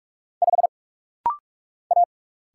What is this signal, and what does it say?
Morse code, heard as a keyed tone, about 46 wpm, 700 Hz. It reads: 5 A
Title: Single Letter-Number - Rapid-Fire 45wpm